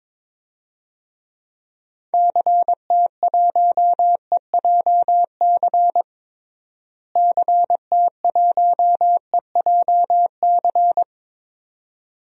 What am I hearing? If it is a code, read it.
Morse code, heard as a keyed tone, about 22 wpm, 700 Hz. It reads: CT1EJC CT1EJC
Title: DXCC-100 Call Signs - 6 Characters - Repeated 2x - 22wpm